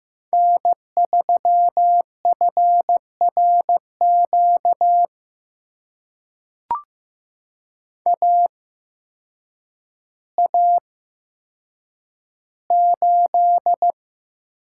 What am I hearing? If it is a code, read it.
Morse code, heard as a keyed tone, about 15 wpm, spaced as for 8 wpm, 700 Hz. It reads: N3FRQ AA8
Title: USA Call Signs - 5 Characters - 8x Character Spacing - 15wpm